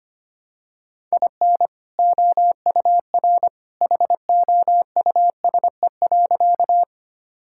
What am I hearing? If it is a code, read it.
Morse code, heard as a keyed tone, about 25 wpm, 700 Hz. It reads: IN OUR HOUSE.
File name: practice / 25wpm / Sentences with 2-Word Phrases - 25wpm